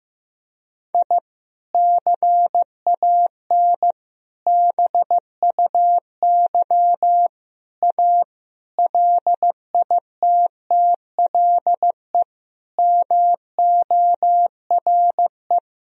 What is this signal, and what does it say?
Morse code, heard as a keyed tone, about 15 wpm, 700 Hz. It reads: I CAN BUY A LITTLE MORE
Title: Sentences from Top 1000 Words - Review Part 1 - 15wpm